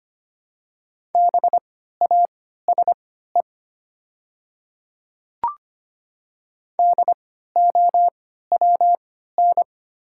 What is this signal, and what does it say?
Morse code, heard as a keyed tone, about 25 wpm, spaced as for 14 wpm, 700 Hz. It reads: BASE DOWN
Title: 3x Character Spacing - Top 300 Words - 25wpm